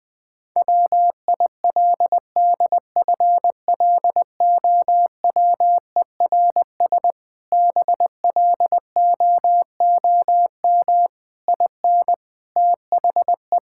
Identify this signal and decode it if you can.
Morse code, heard as a keyed tone, about 21 wpm, 700 Hz. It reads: WILDFLOWERS BLOOM IN THE
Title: Difficult Sentences with Hints - 20wpm